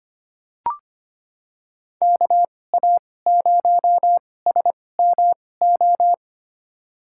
Morse code, heard as a keyed tone, about 25 wpm, spaced as for 18 wpm, 700 Hz. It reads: KA0SMO
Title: USA Call Signs - 6 Characters - 2x Character Spacing - 25wpm